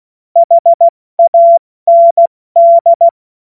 Morse code, heard as a keyed tone, about 16 wpm, 680 Hz. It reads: HAND
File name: Morsekurs / Witze / Witz 24 Juni Mai 2025 70 F80